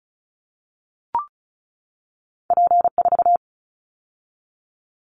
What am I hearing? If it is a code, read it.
Morse code, heard as a keyed tone, about 34 wpm, 700 Hz. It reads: P4
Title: Callsign Prefixes for Top 100 DXCC - 35wpm